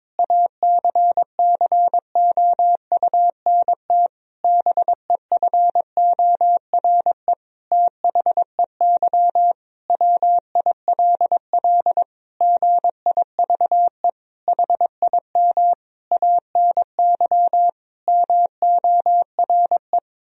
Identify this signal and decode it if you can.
Morse code, heard as a keyed tone, about 21 wpm, 700 Hz. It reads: ACCOUNT BEFORE THEY WILL GIVE HIM ANY MORE
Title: Sentences from Top 1000 Words - Review Part 4 - 22wpm